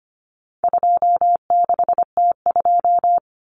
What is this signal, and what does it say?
Morse code, heard as a keyed tone, about 25 wpm, 700 Hz. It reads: 26T2
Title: Common Cut Numbers - 4-Digit 25wpm